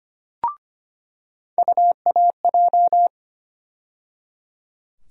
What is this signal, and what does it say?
Morse code, heard as a keyed tone, about 25 wpm, 700 Hz. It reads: UAJ
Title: USA Call Sign Postfixes - 3 Letters 25wpm